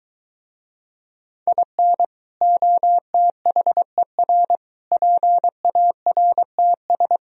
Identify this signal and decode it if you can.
Morse code, heard as a keyed tone, about 24 wpm, 700 Hz. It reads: IN OTHER PARTS